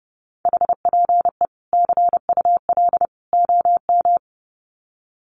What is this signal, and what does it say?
Morse code, heard as a keyed tone, about 30 wpm, 700 Hz. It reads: HPE CUL OM
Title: Common QSO Phrases 30wpm